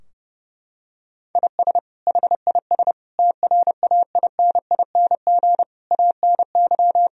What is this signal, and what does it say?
Morse code, heard as a keyed tone, about 32 wpm, 700 Hz. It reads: IS HIS TRAINING ANY
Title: Sentences from Top 1000 Words - Review Part 4 - 30wpm